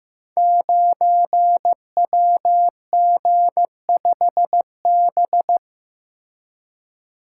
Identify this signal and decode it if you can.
Morse code, heard as a keyed tone, about 15 wpm, 700 Hz. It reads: 9WG5B
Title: Code Groups - 5 Letter:Numbers - 15wpm